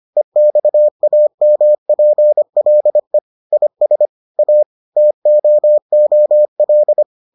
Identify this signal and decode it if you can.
Morse code, heard as a keyed tone, about 25 wpm, 600 Hz. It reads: EXAMPLE IS A TOOL